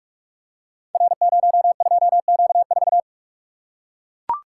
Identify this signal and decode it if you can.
Morse code, heard as a keyed tone, about 44 wpm, 700 Hz. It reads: R02Q4
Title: Code Groups - 5 Letter:Numbers - 45wpm